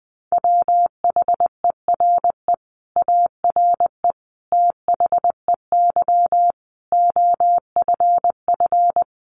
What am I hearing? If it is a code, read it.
Morse code, heard as a keyed tone, about 19 wpm, 700 Hz. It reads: WHERE ARE THEY OFF
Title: Sentences from Top 200 Words 20wpm